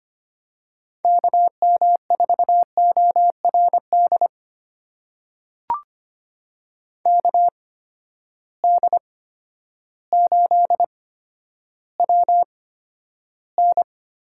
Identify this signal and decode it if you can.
Morse code, heard as a keyed tone, about 25 wpm, spaced as for 11 wpm, 700 Hz. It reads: KM4ORD KD8WN
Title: USA Call Signs - 6 Characters - 8x Character Spacing - 25wpm